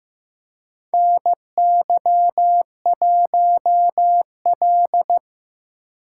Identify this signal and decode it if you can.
Morse code, heard as a keyed tone, about 15 wpm, 700 Hz. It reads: NY1L
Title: USA Call Signs - 4 Characters - 8x Character Spacing - 15wpm